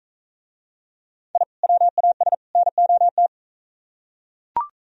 Morse code, heard as a keyed tone, about 41 wpm, 700 Hz. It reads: I WAS NOT